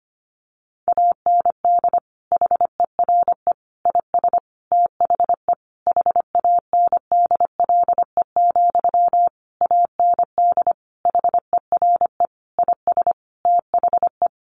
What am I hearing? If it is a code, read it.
Morse code, heard as a keyed tone, about 25 wpm, 700 Hz. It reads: AND HERE IS THE HANDLE, AND HERE IS THE
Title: Sentences from Top 1100 Words 25wpm